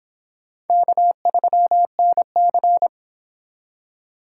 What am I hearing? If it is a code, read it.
Morse code, heard as a keyed tone, about 29 wpm, 700 Hz. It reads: K3NC